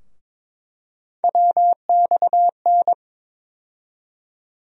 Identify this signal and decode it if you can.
Morse code, heard as a keyed tone, about 22 wpm, 700 Hz. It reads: WXN
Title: USA Call Sign Postfixes - 3 Letters 22wpm